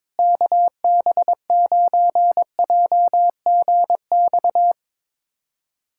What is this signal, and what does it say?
Morse code, heard as a keyed tone, about 22 wpm, 700 Hz. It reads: KB9JGX
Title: USA Callsign DE Callsign 22wpm